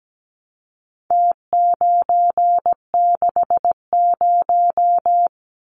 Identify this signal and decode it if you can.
Morse code, heard as a keyed tone, about 17 wpm, 700 Hz. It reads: T960
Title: Numbers - Contest Style - 4-Digit 17wpm